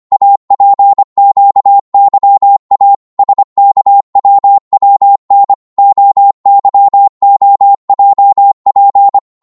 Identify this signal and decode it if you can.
Morse code, heard as a keyed tone, about 25 wpm, 800 Hz. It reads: APQYA SKWWN OYOJP